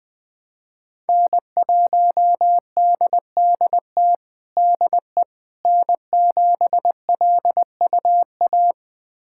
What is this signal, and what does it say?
Morse code, heard as a keyed tone, about 20 wpm, 700 Hz. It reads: N1DDT DE N7LUA
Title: USA Callsign DE Callsign - 5 Characters - 20wpm